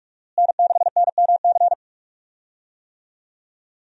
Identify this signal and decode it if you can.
Morse code, heard as a keyed tone, about 46 wpm, 700 Hz. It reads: N6NMC
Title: USA Call Signs - 5 Characters - Repeated 2x - 45wpm